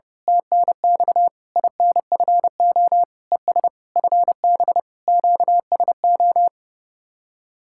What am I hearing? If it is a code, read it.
Morse code, heard as a keyed tone, about 28 wpm, 700 Hz. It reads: TNX INFO ES FB QSO
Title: Speed-Racing Common QSO Phrases - 30wpm